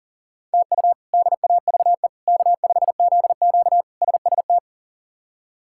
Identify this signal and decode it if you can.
Morse code, heard as a keyed tone, about 40 wpm, 700 Hz. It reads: TU DAVE K5ZQ SST
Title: K1USN SST QSO Practice - 40wpm